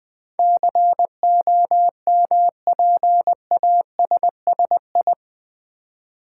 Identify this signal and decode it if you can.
Morse code, heard as a keyed tone, about 20 wpm, 700 Hz. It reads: COMPASSI
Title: Build A Word - 4 Letters At A Time - 10-Letter Words - 20wpm